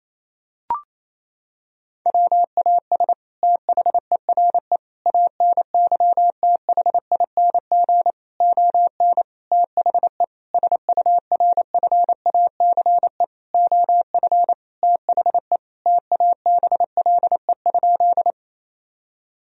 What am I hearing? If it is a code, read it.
Morse code, heard as a keyed tone, about 28 wpm, 700 Hz. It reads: WAS THERE ANYTHING ON THE SURFACE OF THE TABLE?
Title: Sentences from Top 1000 Words - Review Part 3 - 28wpm